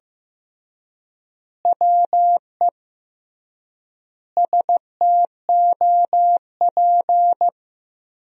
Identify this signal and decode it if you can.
Morse code, heard as a keyed tone, about 15 wpm, 700 Hz. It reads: WE STOP
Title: 3x Word-Spacing - Sentences from Top 300 Words - 15wpm